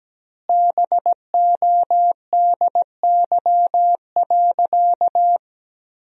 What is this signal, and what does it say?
Morse code, heard as a keyed tone, about 17 wpm, 700 Hz. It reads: BODY.
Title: 2x Word-Spacing - Sentences from Top 400 Words - 17wpm